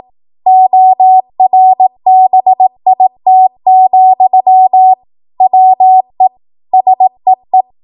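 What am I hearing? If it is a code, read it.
Morse code, heard as a keyed tone, about 18 wpm, 750 Hz. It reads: ORBIT, WE SEE
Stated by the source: Code practice at 18 WPM, a transition run bridging 15 to 20 WPM.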